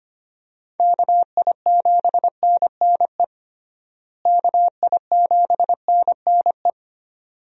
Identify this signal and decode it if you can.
Morse code, heard as a keyed tone, about 25 wpm, 700 Hz. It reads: KI7NNE KI7NNE
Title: USA Call Signs - 6 Characters - Repeated 2x - 25wpm